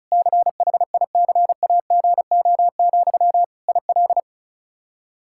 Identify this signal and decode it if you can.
Morse code, heard as a keyed tone, about 36 wpm, 700 Hz. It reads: CHICAGO, IL
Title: K1USN SST QSO Practice 35wpm